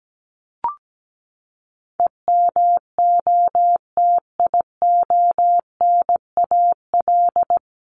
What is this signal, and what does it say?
Morse code, heard as a keyed tone, about 17 wpm, 700 Hz. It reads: EMOTIONAL
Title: Top 501-750 Words 17wpm